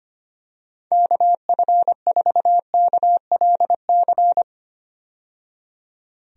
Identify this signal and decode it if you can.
Morse code, heard as a keyed tone, about 25 wpm, 700 Hz. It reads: KF4KLC
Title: USA Call Signs - 6 Characters - 25wpm